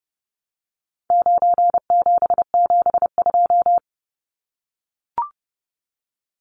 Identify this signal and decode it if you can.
Morse code, heard as a keyed tone, about 30 wpm, 700 Hz. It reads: 9772
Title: Numbers - 4-Digit 30wpm